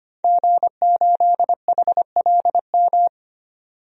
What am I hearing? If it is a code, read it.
Morse code, heard as a keyed tone, about 25 wpm, 700 Hz. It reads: G8HLM
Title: DXCC-100 Call Signs - 5 Characters - 25wpm